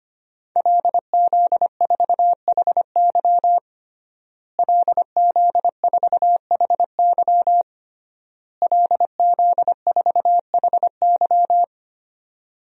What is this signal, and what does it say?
Morse code, heard as a keyed tone, about 25 wpm, 700 Hz. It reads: LZ4HY LZ4HY LZ4HY
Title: DXCC-100 Call Signs - 5 Characters - Repeated 3x - 25wpm